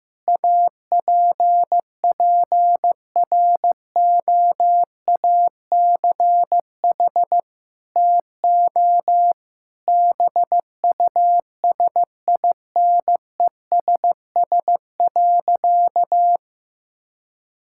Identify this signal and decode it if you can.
Morse code, heard as a keyed tone, about 16 wpm, 700 Hz. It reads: APPROACH TO BUSINESS.
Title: Sentences from Top 1000 Words - Review Part 6 - 15wpm